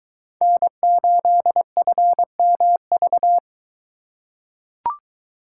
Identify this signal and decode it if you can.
Morse code, heard as a keyed tone, about 21 wpm, 700 Hz. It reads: N8FMV